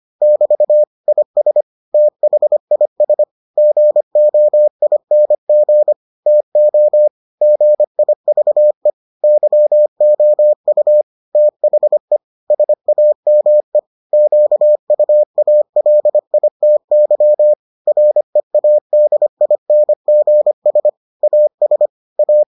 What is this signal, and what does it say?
Morse code, heard as a keyed tone, about 25 wpm, 600 Hz. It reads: <BT> IS THIS GOING TO GIVE YOU THE SAME QUALITY READINGS AS A